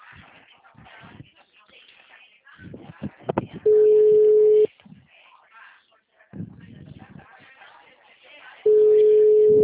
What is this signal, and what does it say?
Ringback tone (Europe, 425 Hz), with speech.
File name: out- -2022